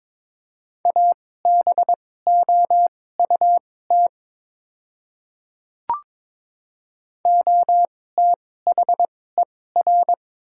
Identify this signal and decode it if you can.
Morse code, heard as a keyed tone, about 23 wpm, spaced as for 16 wpm, 700 Hz. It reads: ABOUT OTHER
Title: 2x Character Spacing - Top 100 Words - 22wpm